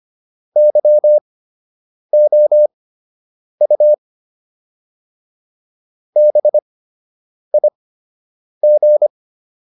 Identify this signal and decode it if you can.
Morse code, heard as a keyed tone, about 25 wpm, spaced as for 8 wpm, 600 Hz. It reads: YOU BIG